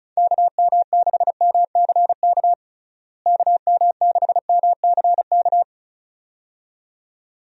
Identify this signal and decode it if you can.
Morse code, heard as a keyed tone, about 36 wpm, 700 Hz. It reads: KM6MCK KM6MCK
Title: USA Call Signs - 6 Characters - Repeated 2x - 35wpm